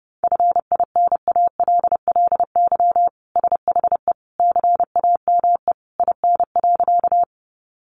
Morse code, heard as a keyed tone, about 30 wpm, 700 Hz. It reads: FINALLY SHE CAME IN.
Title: Sentences from Top 1000 Words 30wpm